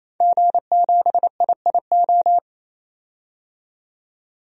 Morse code, heard as a keyed tone, about 27 wpm, 700 Hz. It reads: G7IIO
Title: DXCC-100 Call Signs - 5 Characters - 28wpm